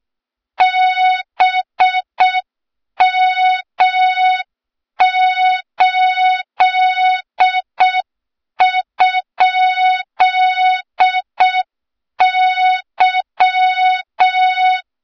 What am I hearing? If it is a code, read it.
Morse code, heard as a keyed tone, about 6 wpm, 3700 Hz, with noise. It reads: BM8?Y
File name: Morze-HA2MN / hang / 23 GYAKORLAT